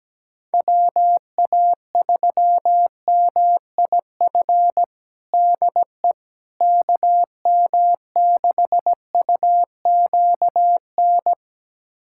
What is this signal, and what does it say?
Morse code, heard as a keyed tone, about 17 wpm, 700 Hz. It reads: WA3MIF DE KM6UQN
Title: USA Callsign DE Callsign - 6 Characters - 17wpm